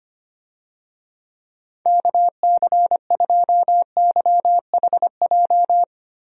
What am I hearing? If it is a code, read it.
Morse code, heard as a keyed tone, about 25 wpm, 700 Hz. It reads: KC2YHJ
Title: USA Call Signs - 6 Characters - Repeated 3x - 25wpm